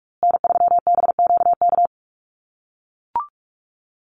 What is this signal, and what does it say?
Morse code, heard as a keyed tone, about 44 wpm, 700 Hz. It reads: N3BQK
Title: USA Call Signs Repeated 2x 45wpm